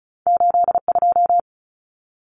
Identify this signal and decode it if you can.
Morse code, heard as a keyed tone, about 34 wpm, 700 Hz. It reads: 82
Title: Numbers - 2-Digit 35wpm